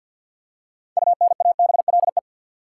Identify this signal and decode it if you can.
Morse code, heard as a keyed tone, about 53 wpm, 700 Hz. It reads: UNABLE
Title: Six Letter Words - 50wpm